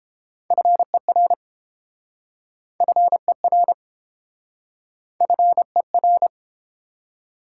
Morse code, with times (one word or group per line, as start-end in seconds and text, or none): 0.5-1.3 FER
2.8-3.7 FER
5.2-6.3 FER